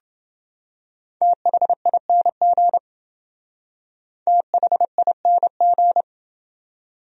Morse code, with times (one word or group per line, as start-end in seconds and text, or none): 1.2-2.8 THING
4.3-6.0 THING